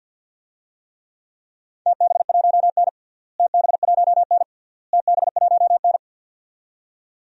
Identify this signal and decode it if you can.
Morse code, heard as a keyed tone, about 46 wpm, 700 Hz. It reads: TB1N TB1N TB1N
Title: DXCC-100 Call Signs - 4 Characters - Repeated 3x - 50wpm